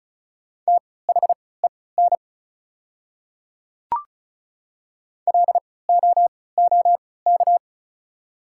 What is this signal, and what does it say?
Morse code, heard as a keyed tone, about 36 wpm, spaced as for 20 wpm, 700 Hz. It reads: THEN LOOK
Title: 3x Character Spacing - Top 100 Words - 35wpm